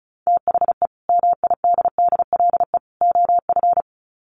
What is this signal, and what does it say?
Morse code, heard as a keyed tone, about 34 wpm, 700 Hz. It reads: THE MIDDLE OF
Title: Sets of 3 Words 35wpm